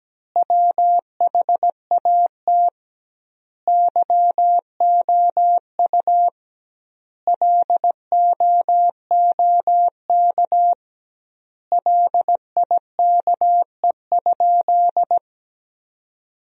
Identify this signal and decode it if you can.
Morse code, heard as a keyed tone, about 17 wpm, 700 Hz. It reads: WHAT YOU LOOK LIKE?
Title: 2x Word-Spacing - Sentences from Top 100 Words - 17wpm